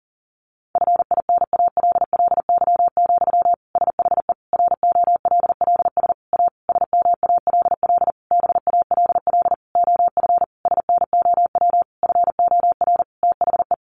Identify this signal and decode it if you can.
Morse code, heard as a keyed tone, about 37 wpm, 700 Hz. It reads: FINALLY, SHE ROLLS A SMALL BALL OF SNOW FOR THE
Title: Easy Phrases 40wpm